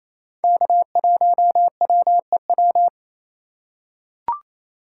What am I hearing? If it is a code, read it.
Morse code, heard as a keyed tone, about 28 wpm, 700 Hz. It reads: K1WEW
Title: USA Call Signs - 4x Character Spacing - 28wpm